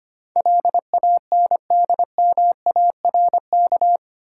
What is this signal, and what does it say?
Morse code, heard as a keyed tone, about 25 wpm, 700 Hz. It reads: LANDMARK
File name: 8 Letter Words - First 3 Letters Given - 25wpm